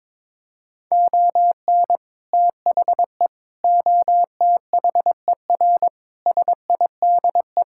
Morse code, heard as a keyed tone, about 22 wpm, 700 Hz. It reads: ON THE OTHER SIDE
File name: practice / 22wpm / Sets of 6 Words - 22wpm